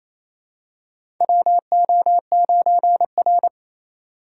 Morse code, heard as a keyed tone, about 28 wpm, 700 Hz. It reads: WO9R
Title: USA Call Signs - 3x Character Spacing - 28wpm